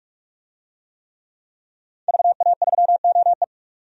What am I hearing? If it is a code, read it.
Morse code, heard as a keyed tone, about 46 wpm, 700 Hz. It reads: VA3OE
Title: DXCC-100 Call Signs - 5 Characters - 45wpm